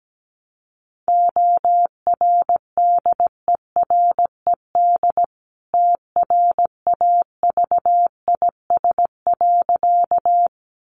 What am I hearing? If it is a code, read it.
Morse code, heard as a keyed tone, about 17 wpm, 700 Hz. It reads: ORDERED TRAVIS.
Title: A Sound Of Thunder 17wpm